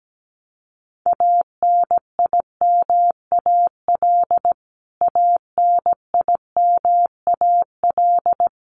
Morse code, heard as a keyed tone, about 17 wpm, 700 Hz. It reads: ANIMAL ANIMAL
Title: Top 500 Words Repeated 17wpm